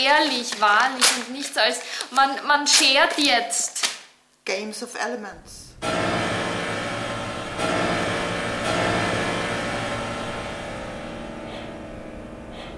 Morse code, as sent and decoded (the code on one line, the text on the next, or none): none